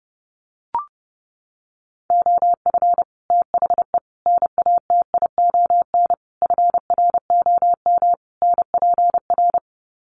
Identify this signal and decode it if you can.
Morse code, heard as a keyed tone, about 30 wpm, 700 Hz. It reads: OF THE NATION FROM NPR
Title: Sets of 5 Words 30wpm